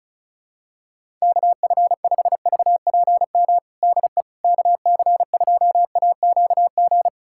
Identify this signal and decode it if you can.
Morse code, heard as a keyed tone, about 34 wpm, 700 Hz. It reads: KF5VPM DE KC2AQG
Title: USA Callsign DE Callsign - 6 Characters - 35wpm